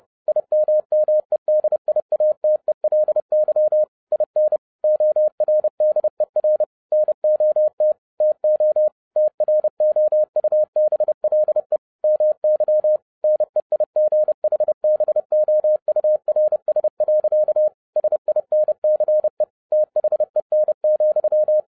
Morse code, heard as a keyed tone, about 30 wpm, 610 Hz. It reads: IMMEDIATELY IN ORDER NOT TO TROUBLE MY NEIGHBOURS. SINCE THEN,
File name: 2011-07-mp3 / n40 EXCITING QRP CW OPERATION 30wpm